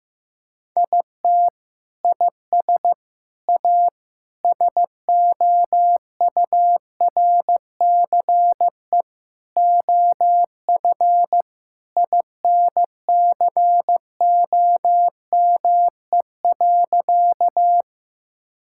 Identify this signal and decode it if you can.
Morse code, heard as a keyed tone, about 16 wpm, 700 Hz. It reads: IT IS A SOURCE OF INCOME.
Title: Sentences from Top 1000 Words - Review Part 5 - 15wpm